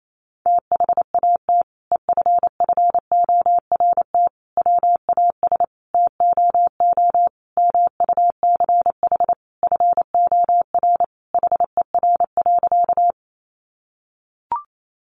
Morse code, heard as a keyed tone, about 28 wpm, 700 Hz. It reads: THAT EFFORT WAS TOO MUCH FOR HER.
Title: Sentences from Top 1200 Words 28wpm